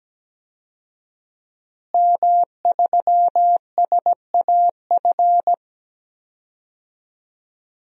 Morse code, heard as a keyed tone, about 17 wpm, 700 Hz. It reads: M3SAF
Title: DXCC-100 Call Signs - 5 Characters - Repeated 3x - 17wpm